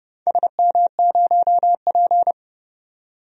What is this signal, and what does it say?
Morse code, heard as a keyed tone, about 30 wpm, 700 Hz. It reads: SM0P
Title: DXCC-100 Call Signs - 4 Characters - 30wpm